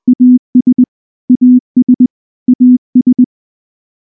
Morse code, heard as a keyed tone, about 19 wpm, 250 Hz. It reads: AS AS AS
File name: morse / morse-training-20WPM-10